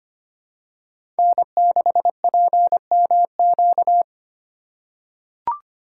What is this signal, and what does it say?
Morse code, heard as a keyed tone, about 25 wpm, 700 Hz. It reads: N6PMQ
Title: USA Call Signs - 5 Characters - 4x Character Spacing - 25wpm